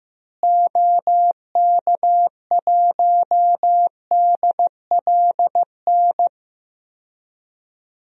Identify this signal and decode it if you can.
Morse code, heard as a keyed tone, about 15 wpm, 700 Hz. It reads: OK1DLN
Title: DXCC-100 Call Signs - 6 Characters - Repeated 3x - 15wpm